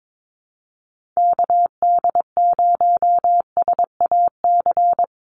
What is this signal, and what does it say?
Morse code, heard as a keyed tone, about 22 wpm, 700 Hz. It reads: KD0SAC
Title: USA Call Signs Repeated 2x 22wpm